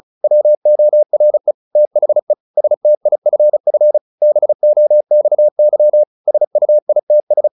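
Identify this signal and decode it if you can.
Morse code, heard as a keyed tone, about 34 wpm, 600 Hz. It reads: WORE THE STIFF BOXY SUITS
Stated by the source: Morse practice at 35 wpm from CW med Gnister, sending Ken Follett's novel Never in English.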